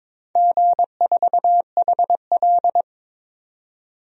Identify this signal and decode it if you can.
Morse code, heard as a keyed tone, about 21 wpm, 700 Hz. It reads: G4HL
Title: DXCC-100 Call Signs - 4 Characters - 22wpm